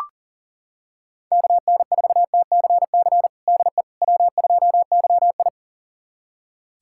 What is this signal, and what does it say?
Morse code, heard as a keyed tone, about 40 wpm, 700 Hz. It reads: KN4TCC DE W2YI
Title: USA Callsign DE Callsign 40wpm